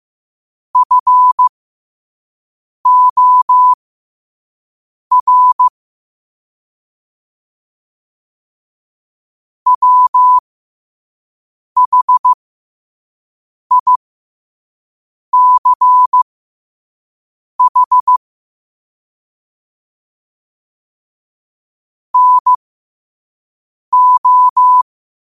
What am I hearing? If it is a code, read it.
Morse code, heard as a keyed tone, about 15 wpm, spaced as for 5 wpm, 1000 Hz. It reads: FOR WHICH NO
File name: qotdfiles / 5wpm